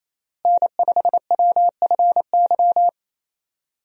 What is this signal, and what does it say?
Morse code, heard as a keyed tone, about 28 wpm, 700 Hz. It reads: N5WFY
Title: USA Call Signs - 5 Characters - Repeated 3x - 28wpm